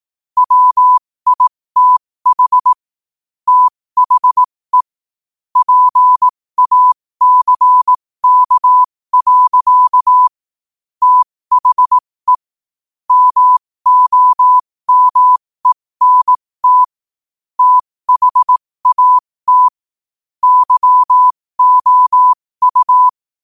Morse code, heard as a keyed tone, about 18 wpm, spaced as for 15 wpm, 1000 Hz. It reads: WITH THE PACK. THE MOMENT THAT YOU
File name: qotdfiles / 15wpm